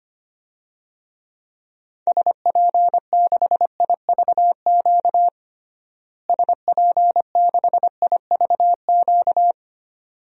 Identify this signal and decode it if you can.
Morse code, heard as a keyed tone, about 25 wpm, 700 Hz. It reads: SP6IVQ SP6IVQ
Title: DXCC-100 Call Signs - 6 Characters - Repeated 2x - 25wpm